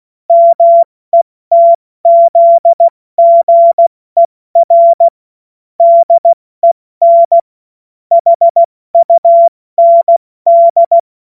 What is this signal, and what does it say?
Morse code, heard as a keyed tone, about 16 wpm, 680 Hz. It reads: METZGER DEN HUND
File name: Morsekurs / Witze / Witz 04. Oktober 2022 70 F80